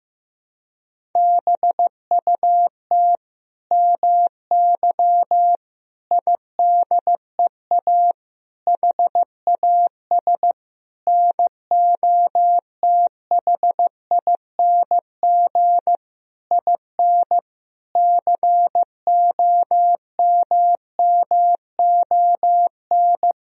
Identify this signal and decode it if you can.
Morse code, heard as a keyed tone, about 15 wpm, 700 Hz. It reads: BUT MY IDEA HAS NOTHING IN COMMON
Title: Sentences from Top 1000 Words - Review Part 3 - 15wpm